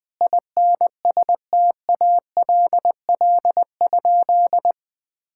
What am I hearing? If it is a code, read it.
Morse code, heard as a keyed tone, about 20 wpm, 700 Hz. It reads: INSTALL?
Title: Sentences from Top 1600 Words 20wpm